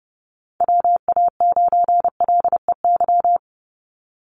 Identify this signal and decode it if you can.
Morse code, heard as a keyed tone, about 29 wpm, 700 Hz. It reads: WA9LEY
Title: Call Signs Repeated 2x - 30wpm